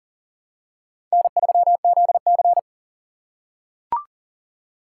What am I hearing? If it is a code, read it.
Morse code, heard as a keyed tone, about 40 wpm, 700 Hz. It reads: N3ZC
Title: USA Call Signs - 4 Characters - 4x Character Spacing - 40wpm